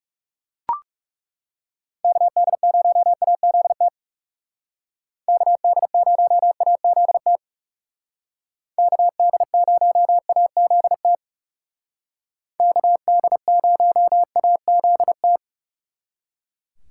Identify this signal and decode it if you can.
Morse code, heard as a keyed tone, about 39 wpm, 700 Hz. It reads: KD0AZT KD0AZT KD0AZT KD0AZT